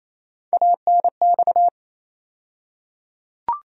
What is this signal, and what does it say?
Morse code, heard as a keyed tone, about 28 wpm, 700 Hz. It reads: ANX
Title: USA Call Sign Postfixes - 3 Letters 28wpm